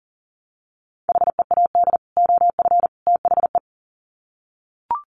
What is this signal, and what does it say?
Morse code, heard as a keyed tone, about 40 wpm, 700 Hz. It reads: HEAD OF THE
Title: Sets of 3 Words 40wpm